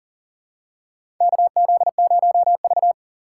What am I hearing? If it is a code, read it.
Morse code, heard as a keyed tone, about 40 wpm, 700 Hz. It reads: KZ0V
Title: USA Call Signs - 4 Characters - 6x Character Spacing - 40wpm